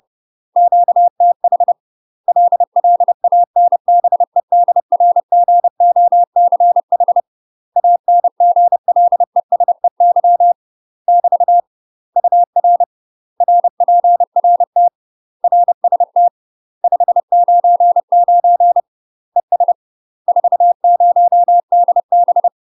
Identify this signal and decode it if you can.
Morse code, heard as a keyed tone, about 30 wpm, 700 Hz. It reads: QTH LLANBEDRGOCH ANGLESEY <BT> UR RPRT RST 599 ES 40DB
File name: UK MASTER QSOs TRAINING / 30WPM / UK MASTER QSOs 0027